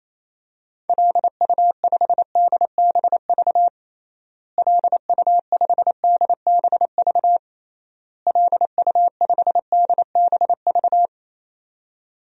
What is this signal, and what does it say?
Morse code, heard as a keyed tone, about 28 wpm, 700 Hz. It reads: LU5DBV LU5DBV LU5DBV
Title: DXCC-100 Call Signs - 6 Characters - Repeated 3x - 28wpm